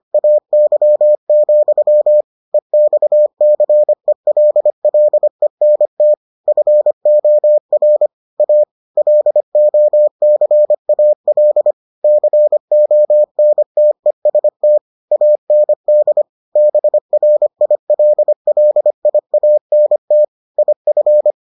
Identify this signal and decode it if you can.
Morse code, heard as a keyed tone, about 25 wpm, 600 Hz. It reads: AY, EXCELLENT FOR A LOCAL CONTEST AND BRILLIANT IF